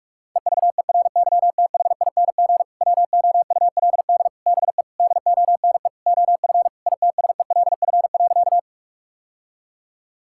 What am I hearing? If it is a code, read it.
Morse code, heard as a keyed tone, about 46 wpm, 700 Hz. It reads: EVERYTHING WOULD BE DONE OF ITSELF.
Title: Sentences from Top 1000 Words - Review Part 5 - 45wpm